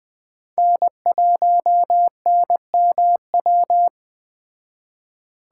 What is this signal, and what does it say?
Morse code, heard as a keyed tone, about 20 wpm, 700 Hz. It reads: N1NMW
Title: Speed Racing - USA 1x3 Call Signs 17wpm